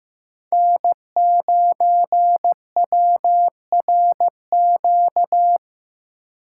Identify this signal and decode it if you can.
Morse code, heard as a keyed tone, about 15 wpm, 700 Hz. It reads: N9WRQ
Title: USA Call Signs - 8x Character Spacing - 15wpm